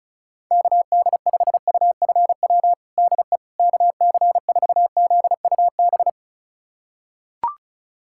Morse code, heard as a keyed tone, about 36 wpm, 700 Hz. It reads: KD5UFW DE KC4ZUB
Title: USA Callsign DE Callsign - 6 Characters - 35wpm